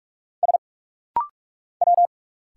Morse code, heard as a keyed tone, about 46 wpm, 700 Hz. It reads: S W
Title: Single Letters - Rapid-Fire 45wpm